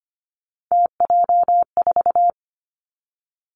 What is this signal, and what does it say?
Morse code, heard as a keyed tone, about 25 wpm, 700 Hz. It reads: TJ4
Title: Three Challenging Letter-Number Combinations 25wpm